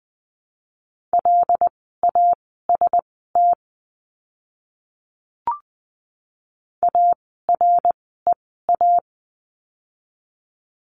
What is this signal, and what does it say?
Morse code, heard as a keyed tone, about 20 wpm, spaced as for 14 wpm, 700 Hz. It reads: LAST AREA
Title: 101 Lesson 17f - Letter u - Words 20wpm